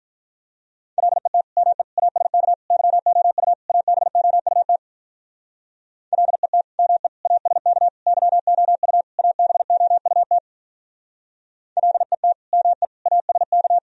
Morse code, heard as a keyed tone, about 45 wpm, 700 Hz. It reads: LET ME ASK YOU ABOUT LET ME ASK YOU ABOUT LET ME ASK